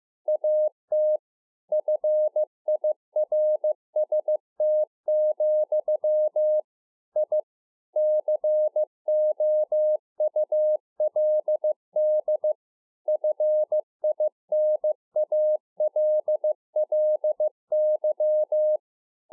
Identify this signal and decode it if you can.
Morse code, heard as a keyed tone, about 15 wpm, 610 Hz. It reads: AT FIRST, I COULD FINALLY
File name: 2011-04-mp3 / A SMALL TRANCEIVER 2 15WPM